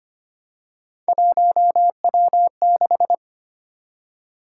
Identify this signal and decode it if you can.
Morse code, heard as a keyed tone, about 25 wpm, 700 Hz. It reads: 1W6
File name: Three Letter-Number Combinations - 25wpm